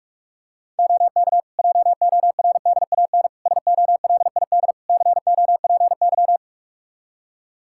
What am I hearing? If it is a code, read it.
Morse code, heard as a keyed tone, about 42 wpm, 700 Hz. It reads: OK JORDAN SOLID COPY
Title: Practice QSO - Exchange 3 - More Info - 45wpm